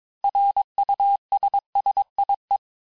Morse code, heard as a keyed tone, about 22 wpm, 800 Hz. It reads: RUSSIE
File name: cours-du-lundi files / cours-1587-texte-clair-vitesse-16mpm-puis-21mpm